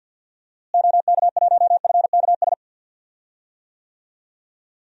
Morse code, heard as a keyed tone, about 47 wpm, 700 Hz. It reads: OK1FKS
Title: DXCC-100 Call Signs - 6 Characters - Repeated 2x - 50wpm